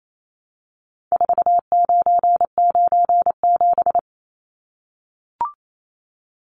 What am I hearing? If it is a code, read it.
Morse code, heard as a keyed tone, about 28 wpm, 700 Hz. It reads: 4997
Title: Numbers - 4-Digit 28wpm